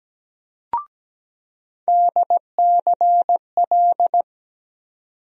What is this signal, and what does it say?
Morse code, heard as a keyed tone, about 17 wpm, 700 Hz. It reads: DCL
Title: USA Call Sign Postfixes - 3 Letters 17wpm